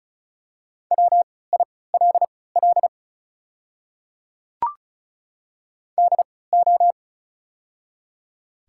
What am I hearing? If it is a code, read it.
Morse code, heard as a keyed tone, about 34 wpm, spaced as for 20 wpm, 690 Hz. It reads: WILL DO
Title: 3x Character Spacing - Top 100 Words - 35wpm